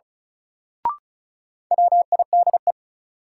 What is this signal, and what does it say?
Morse code, heard as a keyed tone, about 34 wpm, 700 Hz. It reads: WIDE
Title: Sentences from Top 900 Words - New Word Drill - 35wpm